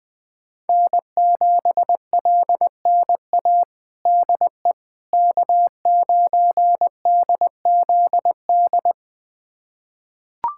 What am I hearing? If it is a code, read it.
Morse code, heard as a keyed tone, about 20 wpm, 700 Hz. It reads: N7LNA DE K9DZD
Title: USA Callsign DE Callsign - 5 Characters - 20wpm